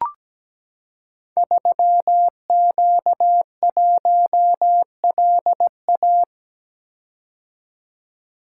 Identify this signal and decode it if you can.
Morse code, heard as a keyed tone, about 17 wpm, 700 Hz. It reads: 3Q1LA
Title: Code Groups - 5 Letter:Numbers - 17wpm